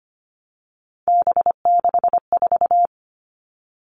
Morse code, heard as a keyed tone, about 25 wpm, 700 Hz. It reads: B64
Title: All Cut Numbers - 3-Digit 25wpm